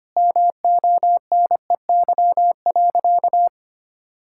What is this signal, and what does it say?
Morse code, heard as a keyed tone, about 25 wpm, 700 Hz. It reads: MONEY.
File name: practice / 25wpm / Difficult Sentences with Hints - 25wpm